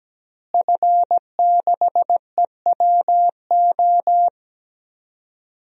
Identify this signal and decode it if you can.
Morse code, heard as a keyed tone, about 17 wpm, 700 Hz. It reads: F6EWO
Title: DXCC-100 Call Signs - 5 Characters - Repeated 3x - 17wpm